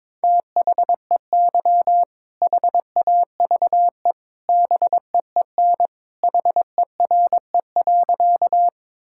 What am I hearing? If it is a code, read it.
Morse code, heard as a keyed tone, about 22 wpm, 700 Hz. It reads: THEY HAVE BEEN HERE.
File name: practice / 22wpm / Sentences with 2-Word Phrases - 22wpm